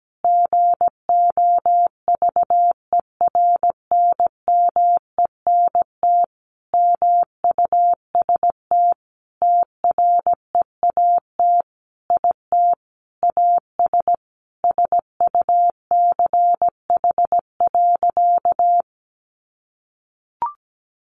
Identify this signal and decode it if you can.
Morse code, heard as a keyed tone, about 17 wpm, 700 Hz. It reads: GOVERNMENT MUST TREAT IT AS SUCH.
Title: Sentences from Top 1300 Words 17wpm